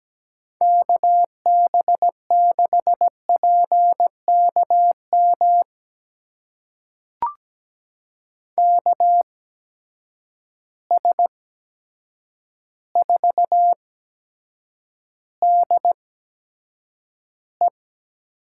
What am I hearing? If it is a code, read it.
Morse code, heard as a keyed tone, about 17 wpm, spaced as for 8 wpm, 700 Hz. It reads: KB6PKM KS4DE
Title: USA Call Signs - 8x Character Spacing - 17wpm